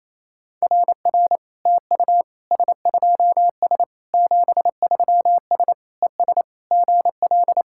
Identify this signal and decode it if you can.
Morse code, heard as a keyed tone, about 28 wpm, 700 Hz. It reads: RR TU S2S 73S ES GL
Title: SOTA QSO Practice 28wpm